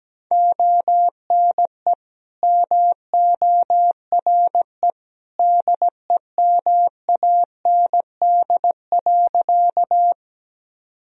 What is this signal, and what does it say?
Morse code, heard as a keyed tone, about 17 wpm, 700 Hz. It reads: ONE MORE DEMAND.
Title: Sentences from Top 1000 Words - Review Part 4 - 17wpm